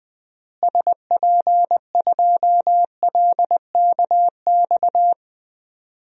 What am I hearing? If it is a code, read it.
Morse code, heard as a keyed tone, about 20 wpm, 700 Hz. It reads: SP2LKX
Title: DXCC-100 Call Signs - 6 Characters - Repeated 3x - 20wpm